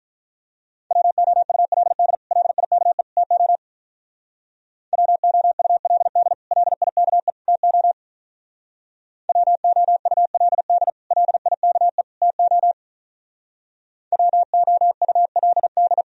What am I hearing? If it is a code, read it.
Morse code, heard as a keyed tone, about 45 wpm, 700 Hz. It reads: WOULD LIKE TO WOULD LIKE TO WOULD LIKE TO WOULD